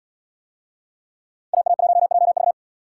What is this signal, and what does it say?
Morse code, heard as a keyed tone, about 69 wpm, 700 Hz. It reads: AA1JV